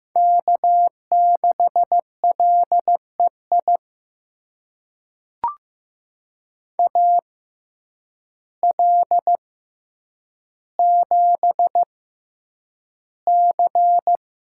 Morse code, heard as a keyed tone, about 15 wpm, spaced as for 8 wpm, 700 Hz. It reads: K6LEI AL7C
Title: USA Call Signs - 5 Characters - 6x Character Spacing - 15wpm